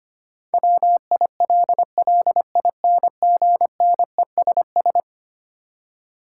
Morse code, heard as a keyed tone, about 25 wpm, 700 Hz. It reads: WILLINGNESS
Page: morse-code-ninja-practice-25wpm